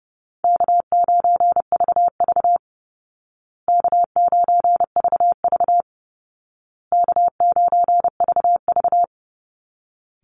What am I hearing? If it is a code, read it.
Morse code, heard as a keyed tone, about 30 wpm, 700 Hz. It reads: K9VV K9VV K9VV
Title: Easy Call Signs Repeated 3x - 30wpm